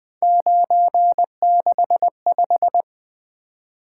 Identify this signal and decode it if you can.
Morse code, heard as a keyed tone, about 23 wpm, 700 Hz. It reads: 965
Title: Speed-Racing - Three Letter-Number Combinations - 20wpm